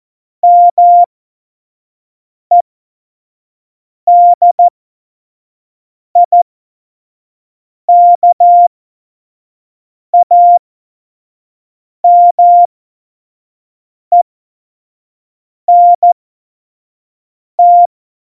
Morse code, heard as a keyed tone, about 14 wpm, spaced as for 5 wpm, 700 Hz. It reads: MEDIKAMENT
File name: Morsekurs / Witze / Witz 20 March 2018, 25 F70